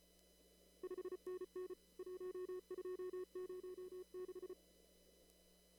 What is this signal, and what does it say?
Morse code, heard as a keyed tone, about 34 wpm, 370 Hz. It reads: 5NN 1206
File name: CW / kt5j wpx cw-op-k5pi